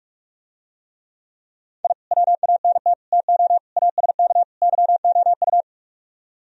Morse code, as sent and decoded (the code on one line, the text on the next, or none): .. / .-- .- -. - / - --- / .- ... -.- / -.-- --- ..-
I WANT TO ASK YOU